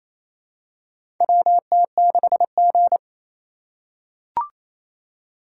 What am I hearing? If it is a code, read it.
Morse code, heard as a keyed tone, about 28 wpm, 700 Hz. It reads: WT6G
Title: USA Call Signs - 2x Character Spacing - 28wpm